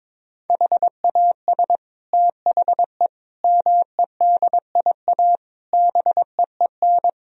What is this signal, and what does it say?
Morse code, heard as a keyed tone, about 22 wpm, 700 Hz. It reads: HAS THE MEDIA BEEN
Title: Sentences from Top 1000 Words - Review Part 4 - 22wpm